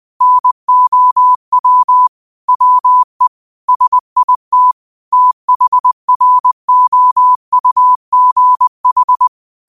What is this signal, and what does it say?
Morse code, heard as a keyed tone, about 20 wpm, 1000 Hz. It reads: NOW WE SIT THROUGH
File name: 20wpm